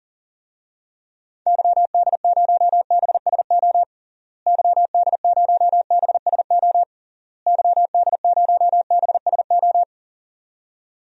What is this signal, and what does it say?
Morse code, heard as a keyed tone, about 40 wpm, 700 Hz. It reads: YD0BSO YD0BSO YD0BSO
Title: DXCC-100 Call Signs - 6 Characters - Repeated 3x - 40wpm